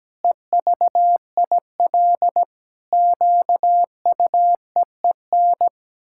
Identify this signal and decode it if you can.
Morse code, heard as a keyed tone, about 17 wpm, 700 Hz. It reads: EVIL QUEEN
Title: Pangram Challenge v2 - 17wpm